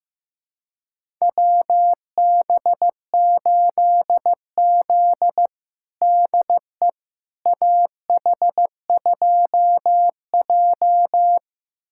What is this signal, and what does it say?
Morse code, heard as a keyed tone, about 15 wpm, 700 Hz. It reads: WB8Z DE AH2J
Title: USA Callsign DE Callsign - 4 Characters - 15wpm